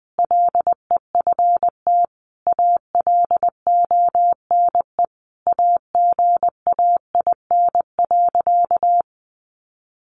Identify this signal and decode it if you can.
Morse code, heard as a keyed tone, about 19 wpm, 700 Hz. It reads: LEFT ALONE AGAIN.
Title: Sentences from Top 1000 Words 20wpm